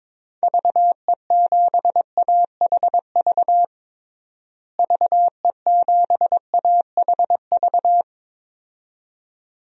Morse code, heard as a keyed tone, about 22 wpm, 700 Hz. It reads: VE7AHV VE7AHV
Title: DXCC-100 Call Signs - 6 Characters - Repeated 2x - 22wpm